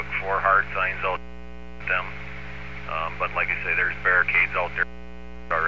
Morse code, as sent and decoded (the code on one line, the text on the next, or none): none